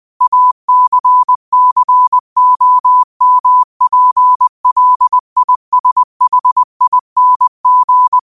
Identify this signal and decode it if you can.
Morse code, heard as a keyed tone, about 20 wpm, 1000 Hz. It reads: ACCOMPLISHING
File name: qotdfiles / 20wpm